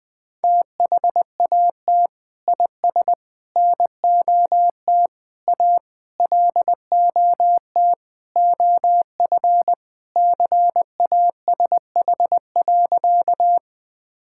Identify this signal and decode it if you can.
Morse code, heard as a keyed tone, about 20 wpm, 700 Hz. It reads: THAT IS NOT A LOT OF CASH.
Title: Sentences from Top 1000 Words - Review Part 5 - 20wpm